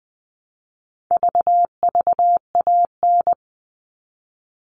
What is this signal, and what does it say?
Morse code, heard as a keyed tone, about 20 wpm, 700 Hz. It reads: VVAN
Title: All Cut Numbers - 4-Digit 20wpm